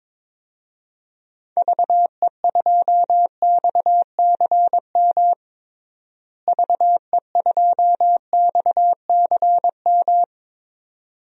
Morse code, heard as a keyed tone, about 22 wpm, 700 Hz. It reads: VE2XCM VE2XCM
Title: DXCC-100 Call Signs - 6 Characters - Repeated 2x - 22wpm